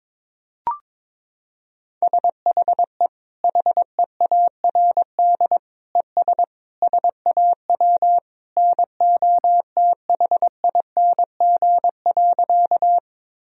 Morse code, heard as a keyed tone, about 22 wpm, 700 Hz. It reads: SHE HEARD ES SAW NOTHING.
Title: Sentences from Top 500 Words and Abbreviations 22wpm